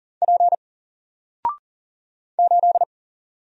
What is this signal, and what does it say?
Morse code, heard as a keyed tone, about 39 wpm, 700 Hz. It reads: P 8
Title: Single Character (Letter:Number:Punctuation:Prosign) 40wpm